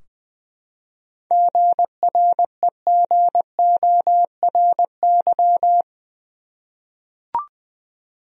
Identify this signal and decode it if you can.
Morse code, heard as a keyed tone, about 21 wpm, 700 Hz. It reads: GREGORY
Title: USA Common Names - 20wpm